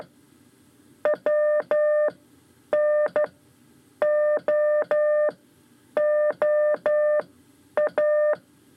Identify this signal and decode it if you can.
Morse code, heard as a keyed tone, about 12 wpm, spaced as for 8 wpm, 600 Hz. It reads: WNOOA
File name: 5 Lezione CW 2